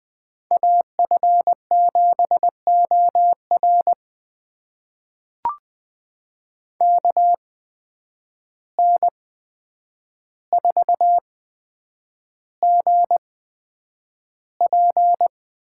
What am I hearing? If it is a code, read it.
Morse code, heard as a keyed tone, about 20 wpm, spaced as for 9 wpm, 700 Hz. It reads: AF7OR KN4GP
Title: USA Call Signs - 5 Characters - 8x Character Spacing - 20wpm